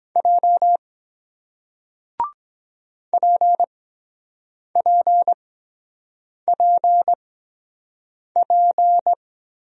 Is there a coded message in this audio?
Yes: Morse code from 0.2 to 9.1 s, heard as a keyed tone, about 23 wpm, spaced as for 7 wpm, 700 Hz. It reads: J PPPP